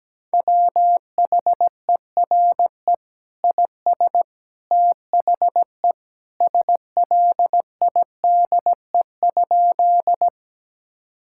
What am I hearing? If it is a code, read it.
Morse code, heard as a keyed tone, about 17 wpm, 700 Hz. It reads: WHERE IS THE SLIDE?
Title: Sentences from Top 1600 Words 17wpm